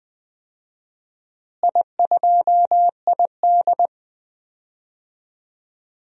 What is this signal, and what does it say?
Morse code, heard as a keyed tone, about 20 wpm, 700 Hz. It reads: I2ID
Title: DXCC-100 Call Signs - 4 Characters - Repeated 3x - 20wpm